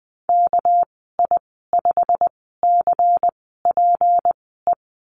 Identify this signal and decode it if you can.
Morse code, heard as a keyed tone, about 20 wpm, spaced as for 14 wpm, 700 Hz. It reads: KI5CPE
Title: 213 Lesson 34g - Number 8 - Call Signs 20wpm